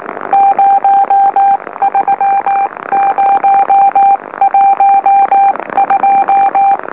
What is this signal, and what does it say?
Morse code, heard as a keyed tone, about 19 wpm, spaced as for 16 wpm, 790 Hz, with radio noise. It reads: 03012
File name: M04- -2345utc-LOLO